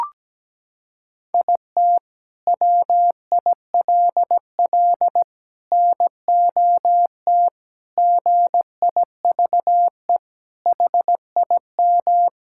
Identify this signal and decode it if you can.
Morse code, heard as a keyed tone, about 17 wpm, 700 Hz. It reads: IT WILL NOT GIVE HIM
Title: Sentences from Top 1600 Words 17wpm